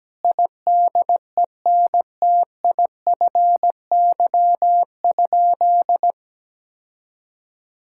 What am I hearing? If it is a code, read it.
Morse code, heard as a keyed tone, about 17 wpm, 700 Hz. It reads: IDENTIFY?
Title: Sentences from Top 1000 Words - Review Part 6 - 17wpm